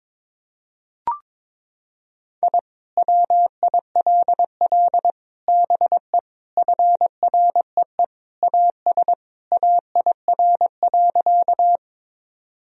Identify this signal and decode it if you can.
Morse code, heard as a keyed tone, about 22 wpm, 700 Hz. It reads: I WILL BE FREE AS AIR.
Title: Sentences from Top 1000 Words - Review Part 1 - 22wpm